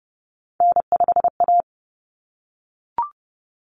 Morse code, heard as a keyed tone, about 30 wpm, 700 Hz. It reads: N5A
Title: Common Cut Numbers - 3-Digit 30wpm